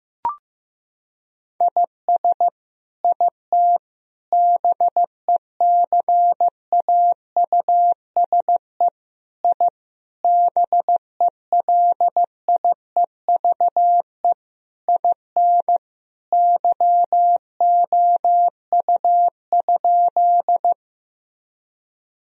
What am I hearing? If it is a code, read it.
Morse code, heard as a keyed tone, about 15 wpm, 700 Hz. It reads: IS IT BECAUSE I BELIEVE IN YOU?
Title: Sentences from Top 1000 Words - Review Part 3 - 15wpm